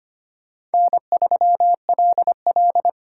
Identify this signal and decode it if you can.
Morse code, heard as a keyed tone, about 32 wpm, 700 Hz. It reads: N3LL